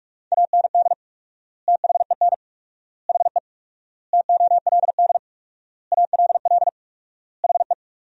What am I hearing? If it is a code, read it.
Morse code, heard as a keyed tone, about 41 wpm, 700 Hz. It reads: AND THEN HE TOLD ALL HE
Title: 4x Word-Spacing - Sentences from Top 400 Words - 45wpm